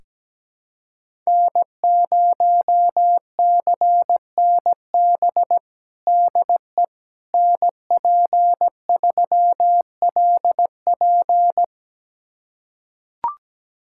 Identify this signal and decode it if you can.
Morse code, heard as a keyed tone, about 17 wpm, 700 Hz. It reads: N0CNB DE NP3LP
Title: USA Callsign DE Callsign - 5 Characters - 17wpm